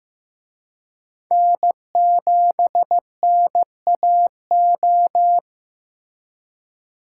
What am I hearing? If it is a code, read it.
Morse code, heard as a keyed tone, about 18 wpm, 700 Hz. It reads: N7NAO
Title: Speed Racing - USA 1x3 Call Signs 15wpm